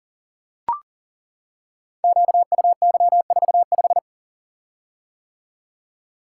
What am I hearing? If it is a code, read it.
Morse code, heard as a keyed tone, about 38 wpm, 700 Hz. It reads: QUY45
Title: Code Groups - 5 Letter:Numbers - 40wpm